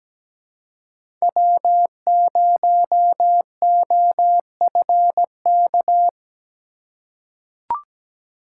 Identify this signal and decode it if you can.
Morse code, heard as a keyed tone, about 17 wpm, 700 Hz. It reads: W0OFK
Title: USA Call Signs - 5 Characters - 4x Character Spacing - 17wpm